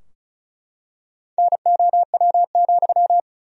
Morse code, heard as a keyed tone, about 35 wpm, 700 Hz. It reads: NOW,
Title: 2x Word-Spacing - Sentences from Top 400 Words - 35wpm